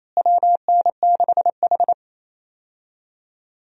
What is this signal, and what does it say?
Morse code, heard as a keyed tone, about 28 wpm, 700 Hz. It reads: WN6H
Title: USA Call Signs - 4 Characters - Repeated 2x - 28wpm